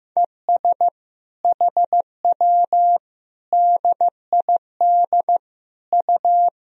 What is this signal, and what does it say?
Morse code, heard as a keyed tone, about 15 wpm, 700 Hz. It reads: ES HW DID U
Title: Sentences from Top 100 Words and Abbreviations 15wpm